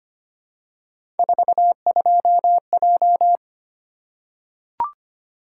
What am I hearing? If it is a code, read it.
Morse code, heard as a keyed tone, about 25 wpm, 700 Hz. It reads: 42J
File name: Three Letter-Number Combinations - 25wpm